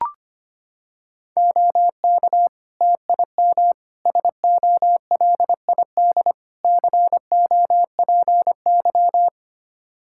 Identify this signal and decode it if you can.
Morse code, heard as a keyed tone, about 25 wpm, 700 Hz. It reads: OK TIM SOLID COPY
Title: Practice QSO - Exchange 3 - More Info - 25wpm